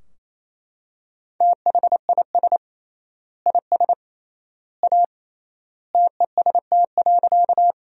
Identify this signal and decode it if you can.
Morse code, heard as a keyed tone, about 28 wpm, 700 Hz. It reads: THIS IS A TEST.
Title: 3x Word-Spacing - Sentences from Top 500 Words - 28wpm